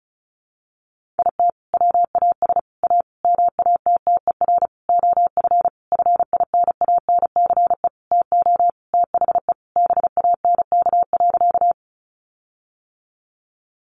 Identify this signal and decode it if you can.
Morse code, heard as a keyed tone, about 34 wpm, 700 Hz. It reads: IT WAS A MATTER OF FINANCE TO THE BANK.
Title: Sentences from Top 1500 Words 35wpm